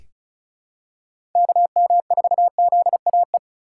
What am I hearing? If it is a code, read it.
Morse code, heard as a keyed tone, about 36 wpm, 700 Hz. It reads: KM4ZAE
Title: USA Call Signs - 6 Characters - 2x Character Spacing - 35wpm